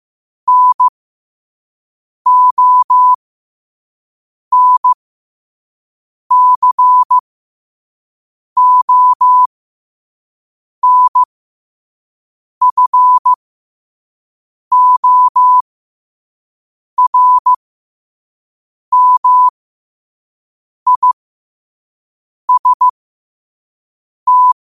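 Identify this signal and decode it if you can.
Morse code, heard as a keyed tone, about 15 wpm, spaced as for 5 wpm, 1000 Hz. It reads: NONCONFORMIST
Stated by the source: Morse code sent at 5 wpm.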